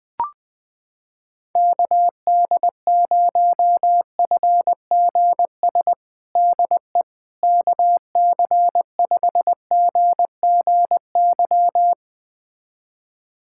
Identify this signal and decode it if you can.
Morse code, heard as a keyed tone, about 21 wpm, 700 Hz. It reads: KD0FGS DE KC5GGY
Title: USA Callsign DE Callsign - 6 Characters - 20wpm